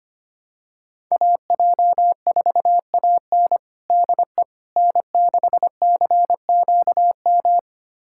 Morse code, heard as a keyed tone, about 25 wpm, 700 Hz. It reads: AJ4AN DE N6CQM
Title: USA Callsign DE Callsign - 5 Characters - 25wpm